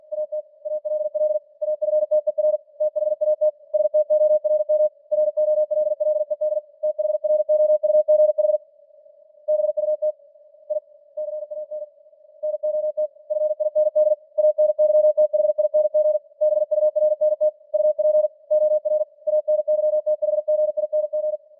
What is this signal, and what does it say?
Morse code, heard as a keyed tone, about 48 wpm, 610 Hz. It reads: AT ALL AFTER THAT STORM ROLLED THROUGH BUT I CAN NOT FIND ANYTHING BURNT UP OR ANYTHGING